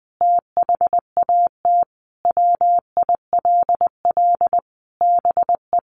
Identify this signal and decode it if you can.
Morse code, heard as a keyed tone, about 20 wpm, 700 Hz. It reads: THAT WILL BE
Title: Sentences from Top 800 Words 20wpm